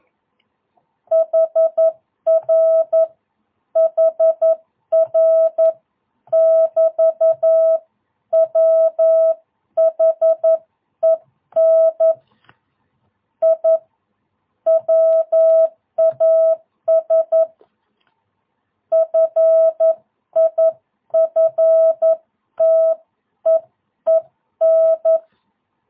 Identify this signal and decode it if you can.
Morse code, heard as a keyed tone, about 11 wpm, 650 Hz. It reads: HR HR <BT> WHEN I WAS FIFTEEN
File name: jh8cbh / 2022cw